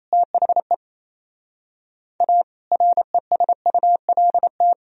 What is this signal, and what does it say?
Morse code, heard as a keyed tone, about 35 wpm, 700 Hz. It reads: THE A RESULT
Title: Speed-Racing - Sets of 4 Words 28wpm